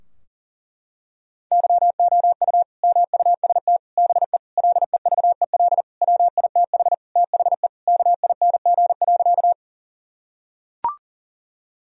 Morse code, heard as a keyed tone, about 40 wpm, 700 Hz. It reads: YOU MUST BE LEVEL WITH THE KING.
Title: Sentences from Top 1000 Words - Review Part 4 - 40wpm